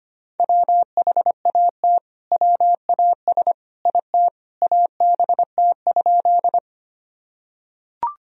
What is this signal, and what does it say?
Morse code, heard as a keyed tone, about 25 wpm, 700 Hz. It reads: WHAT WAS IT ABT?
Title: Sentences from Top 100 Words and Abbreviations 25wpm